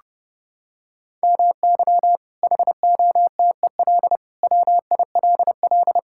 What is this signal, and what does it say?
Morse code, heard as a keyed tone, about 32 wpm, 700 Hz. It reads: MY HOTEL WILL
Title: Sentences from Top 1600 Words 30wpm